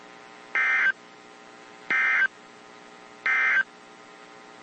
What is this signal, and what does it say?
Emergency Alert System (SAME) tones, heard as harsh buzzing bursts. It is heard from 0.4 to 3.7 s.
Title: WZ2523 - NOAA Weather Radio